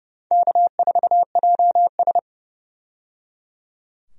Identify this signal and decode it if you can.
Morse code, heard as a keyed tone, about 29 wpm, 700 Hz. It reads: K4JS
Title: USA Call Signs - 4 Characters - 30wpm